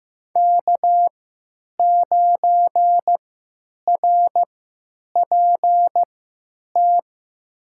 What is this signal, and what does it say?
Morse code, heard as a keyed tone, about 15 wpm, spaced as for 8 wpm, 700 Hz. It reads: K9RPT
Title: USA Call Signs - 5 Characters - 3x Character Spacing - 15wpm